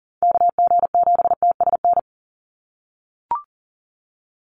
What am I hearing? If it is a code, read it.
Morse code, heard as a keyed tone, about 40 wpm, 700 Hz. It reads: KG7TSN
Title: USA Call Signs 40wpm